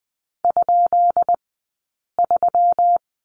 Morse code, heard as a keyed tone, about 20 wpm, 700 Hz. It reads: ? 3
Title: 165 Lesson 27d - Number 7 - Two Characters 20wpm